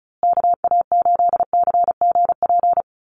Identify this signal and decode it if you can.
Morse code, heard as a keyed tone, about 34 wpm, 700 Hz. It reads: KA8CGP
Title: USA Call Signs Repeated 2x 35wpm